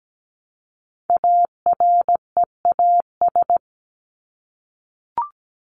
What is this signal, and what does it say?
Morse code, heard as a keyed tone, about 17 wpm, 700 Hz. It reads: AREAS
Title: ES:S Suffix - Top 100 Words with Base Words 17wpm